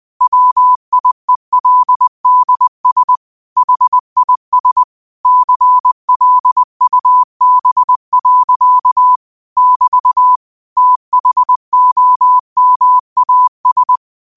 Morse code, heard as a keyed tone, about 20 wpm, 1000 Hz. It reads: WIELDS HIS CLUB. <BT> THOMAS
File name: qotdfiles / 20wpm